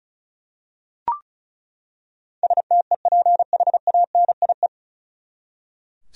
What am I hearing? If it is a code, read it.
Morse code, heard as a keyed tone, about 36 wpm, 700 Hz. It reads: STEPHANIE
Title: USA Common Names - 35wpm